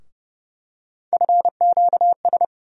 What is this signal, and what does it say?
Morse code, heard as a keyed tone, about 30 wpm, 700 Hz. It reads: FQS
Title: USA Call Sign Postfixes - 3 Letters 30wpm